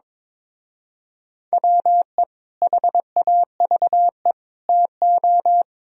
Morse code, heard as a keyed tone, about 25 wpm, 700 Hz. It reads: WE HAVE TO
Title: Speed-Racing - Sets of 3 Words 22wpm